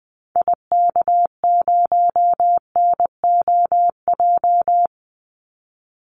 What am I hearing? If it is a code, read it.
Morse code, heard as a keyed tone, about 20 wpm, 700 Hz. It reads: IK0NOJ
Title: Call Signs Repeated 2x - 20wpm